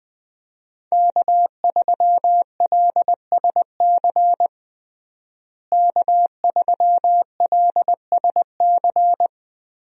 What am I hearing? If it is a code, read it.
Morse code, heard as a keyed tone, about 21 wpm, 700 Hz. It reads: K3LSC K3LSC
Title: USA Call Signs - 5 Characters - Repeated 3x - 20wpm